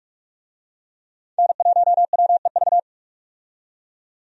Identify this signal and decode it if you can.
Morse code, heard as a keyed tone, about 46 wpm, 700 Hz. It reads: N1WEV
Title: USA Call Signs - 5 Characters - 45wpm